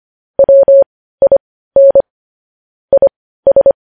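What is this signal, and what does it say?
Morse code, heard as a keyed tone, about 25 wpm, spaced as for 15 wpm, 550 Hz. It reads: WIN IS
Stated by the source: Morse code news headlines at 15 wpm.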